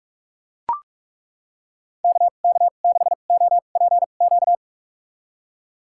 Morse code, heard as a keyed tone, about 41 wpm, spaced as for 31 wpm, 700 Hz. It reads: KK6OPQ
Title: USA Call Signs - 2x Character Spacing - 45wpm